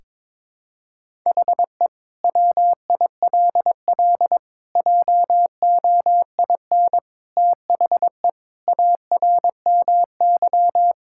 Morse code, heard as a keyed tone, about 22 wpm, 700 Hz. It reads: HE WILL JOIN THE ARMY
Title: Sentences from Top 1000 Words - Review Part 6 - 22wpm